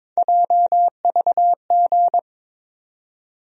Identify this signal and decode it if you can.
Morse code, heard as a keyed tone, about 22 wpm, 700 Hz. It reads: JVG
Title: USA Call Sign Postfixes - 3 Letters 22wpm